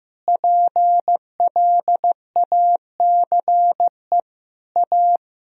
Morse code, heard as a keyed tone, about 15 wpm, 700 Hz. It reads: PLACE A
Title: Sentences from Top 1000 Words - Review Part 2 - 15wpm